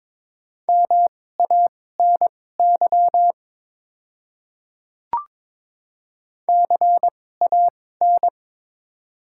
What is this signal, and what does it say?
Morse code, heard as a keyed tone, about 22 wpm, spaced as for 16 wpm, 700 Hz. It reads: MANY CAN
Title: 2x Character Spacing - Top 200 Words - 22wpm